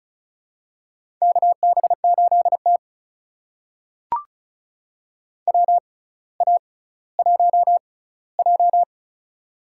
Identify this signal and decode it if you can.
Morse code, heard as a keyed tone, about 36 wpm, spaced as for 18 wpm, 700 Hz. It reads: KB8T WA1J
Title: USA Call Signs - 4 Characters - 6x Character Spacing - 35wpm